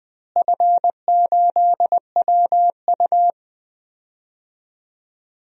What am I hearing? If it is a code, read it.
Morse code, heard as a keyed tone, about 20 wpm, 700 Hz. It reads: F8WU
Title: DXCC-100 Call Signs - 4 Characters - Repeated 3x - 20wpm